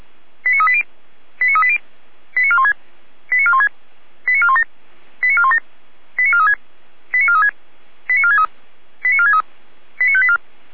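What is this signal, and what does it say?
Five-tone selective call (ZVEI), heard as quick runs of five tones, tones of about 70 ms. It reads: 89380 89380 89416 89416 89417 89417 89436 89436 89463 89463 89584
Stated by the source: ZVEI five-tone selective call received with SDR.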